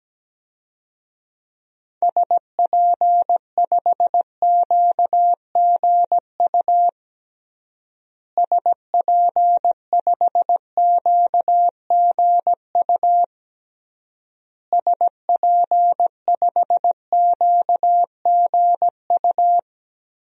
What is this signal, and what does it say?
Morse code, heard as a keyed tone, about 17 wpm, 700 Hz. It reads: SP5QGU SP5QGU SP5QGU
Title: DXCC-100 Call Signs - 6 Characters - Repeated 3x - 17wpm